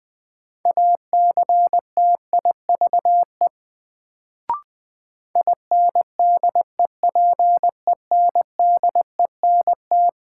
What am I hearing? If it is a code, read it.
Morse code, heard as a keyed tone, about 20 wpm, 700 Hz. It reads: ACTIVE INDEPENDENT
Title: Sentences from Top 900 Words - New Word Drill - 20wpm